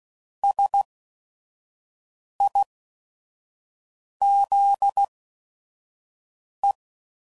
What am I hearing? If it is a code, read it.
Morse code, heard as a keyed tone, about 16 wpm, spaced as for 5 wpm, 800 Hz. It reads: SIZE
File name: CW Code Groups @5WPM